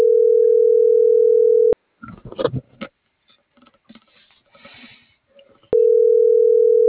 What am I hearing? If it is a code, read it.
Ringback tone (North America), 440 + 480 Hz.
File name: CENTRAL SKYBLUE / out francisco.chavez